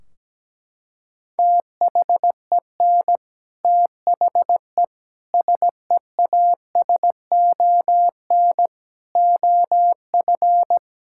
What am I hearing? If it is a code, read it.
Morse code, heard as a keyed tone, about 17 wpm, 700 Hz. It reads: THEN THE SEASON OF
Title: Sentences from Top 1600 Words 17wpm